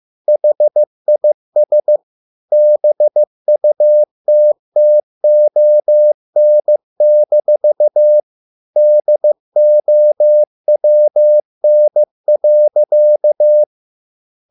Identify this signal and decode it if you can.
Morse code, heard as a keyed tone, about 15 wpm, 600 Hz. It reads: HIS BUTTON-DOWN.